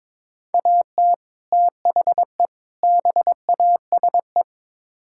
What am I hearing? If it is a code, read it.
Morse code, heard as a keyed tone, about 22 wpm, 700 Hz. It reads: AT THE BASE
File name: practice / 22wpm / Build A Phrase - 5-Word Phrases - 22wpm